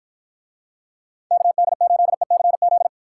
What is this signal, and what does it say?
Morse code, heard as a keyed tone, about 53 wpm, 700 Hz. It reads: KD8ECZ